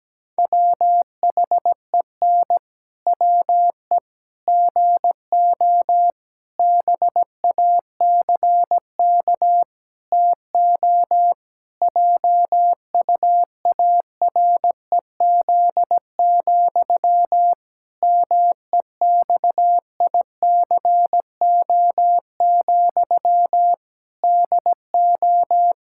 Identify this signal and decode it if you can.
Morse code, heard as a keyed tone, about 17 wpm, 700 Hz. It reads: WHEN WE GO BACK TO JUAREZ, MEXICO, DO
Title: Pangram Challenge v2 - 17wpm